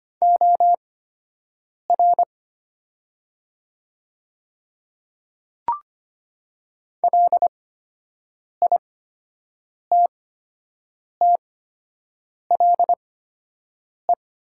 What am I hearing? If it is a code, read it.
Morse code, heard as a keyed tone, about 25 wpm, spaced as for 7 wpm, 700 Hz. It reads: OR LITTLE
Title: 8x Character Spacing - Top 500 Words - 25wpm